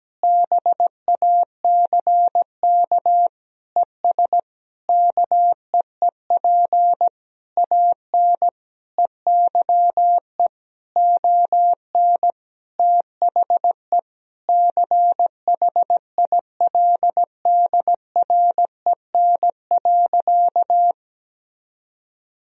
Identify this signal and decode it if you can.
Morse code, heard as a keyed tone, about 17 wpm, 700 Hz. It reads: BACK ES KEEP AN EYE ON THE CHILDREN.
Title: Sentences from Top 300 Words and Abbreviations 17wpm